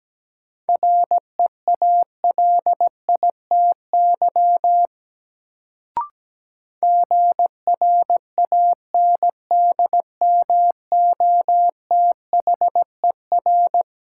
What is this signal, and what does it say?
Morse code, heard as a keyed tone, about 17 wpm, 700 Hz. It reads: REALITY GRANDMOTHER
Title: Sentences from Top 1200 Words - New Word Drill Alt - 17wpm